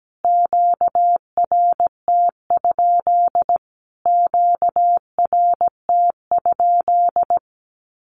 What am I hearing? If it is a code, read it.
Morse code, heard as a keyed tone, about 17 wpm, 700 Hz. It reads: QRT? QRT?
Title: Q-Codes Repeated 3x 17wpm